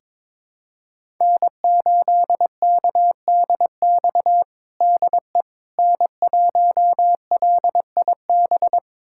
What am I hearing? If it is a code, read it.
Morse code, heard as a keyed tone, about 22 wpm, 700 Hz. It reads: N8KDX DE N1LIB
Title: USA Callsign DE Callsign - 5 Characters - 22wpm